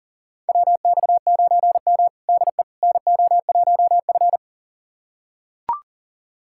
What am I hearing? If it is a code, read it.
Morse code, heard as a keyed tone, about 40 wpm, 700 Hz. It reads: WX9M DE NO1F
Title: USA Callsign DE Callsign - 4 Characters - 40wpm